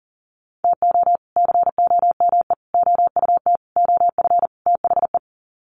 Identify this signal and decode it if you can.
Morse code, heard as a keyed tone, about 40 wpm, 700 Hz. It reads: TO COME OUT OF THE
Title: Sets of 5 Words Encore2 40wpm